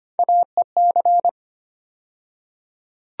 Morse code, heard as a keyed tone, about 25 wpm, 700 Hz. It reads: AEC
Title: USA Call Sign Postfixes - 3 Letters 25wpm